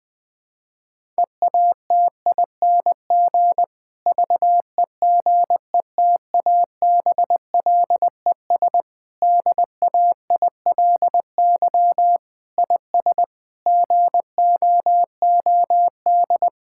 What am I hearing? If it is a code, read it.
Morse code, heard as a keyed tone, about 20 wpm, 700 Hz. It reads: EATING VEGETABLES DAILY IS GOOD
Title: Difficult Sentences with Hints - 20wpm